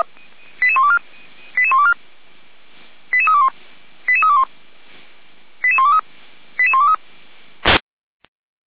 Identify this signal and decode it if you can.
Five-tone selective call (ZVEI), heard as quick runs of five tones, tones of about 70 ms. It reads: 88124 88124 88321 88321 88123 88123